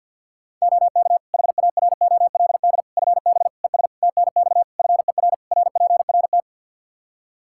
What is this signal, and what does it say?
Morse code, heard as a keyed tone, about 52 wpm, 700 Hz. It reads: OK HAROLD FB ES TNX FER RPRT
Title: Practice QSO - Exchange 2 - Info - 50wpm